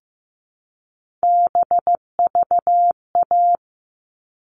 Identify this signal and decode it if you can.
Morse code, heard as a keyed tone, about 15 wpm, 700 Hz. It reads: BVA
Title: All Cut Numbers - 3-Digit 15wpm